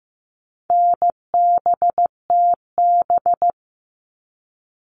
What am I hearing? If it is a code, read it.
Morse code, heard as a keyed tone, about 15 wpm, 700 Hz. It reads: NBTB
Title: All Cut Numbers - 4-Digit 15wpm